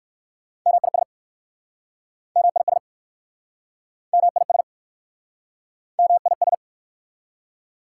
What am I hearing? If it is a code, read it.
Morse code, heard as a keyed tone, about 60 wpm, 690 Hz. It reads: MIS MIS MIS MIS